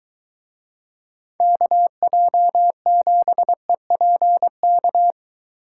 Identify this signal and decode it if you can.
Morse code, heard as a keyed tone, about 23 wpm, 700 Hz. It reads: KJ7EPK